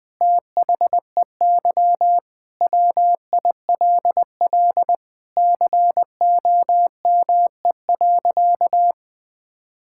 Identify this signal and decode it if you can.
Morse code, heard as a keyed tone, about 21 wpm, 700 Hz. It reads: THEY WILL COME.
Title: Sentences with 7-Word Phrases - 20wpm